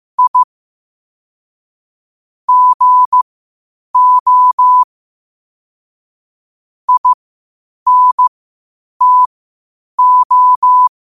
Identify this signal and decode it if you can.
Morse code, heard as a keyed tone, about 15 wpm, spaced as for 8 wpm, 1000 Hz. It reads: I GO INTO
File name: qotdfiles / 8wpm